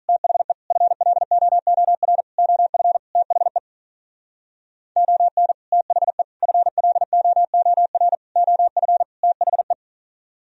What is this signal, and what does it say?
Morse code, heard as a keyed tone, about 42 wpm, 700 Hz. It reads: THE FLOOR OF THE ON THE FLOOR OF THE
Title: Speed-Racing - Sets of 5 Words 35wpm